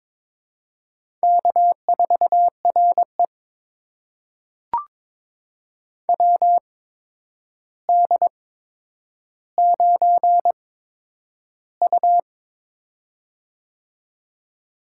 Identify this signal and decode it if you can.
Morse code, heard as a keyed tone, about 22 wpm, spaced as for 9 wpm, 700 Hz. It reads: K4RE WD9U
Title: USA Call Signs - 4 Characters - 8x Character Spacing - 22wpm